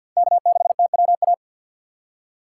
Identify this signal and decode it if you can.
Morse code, heard as a keyed tone, about 50 wpm, 700 Hz. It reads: K6TWA
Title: USA Call Signs - 5 Characters - 4x Character Spacing - 50wpm